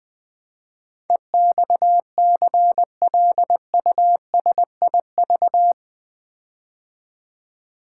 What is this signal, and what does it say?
Morse code, heard as a keyed tone, about 20 wpm, 700 Hz. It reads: EXCLUSIV
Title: Build A Word - 4 Letters At A Time - 11-Letter Words - 20wpm